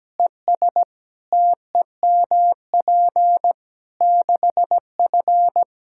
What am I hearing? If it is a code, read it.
Morse code, heard as a keyed tone, about 17 wpm, 700 Hz. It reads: ES TEMP 6F
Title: Practice QSO - Exchange 2 - Info - 17wpm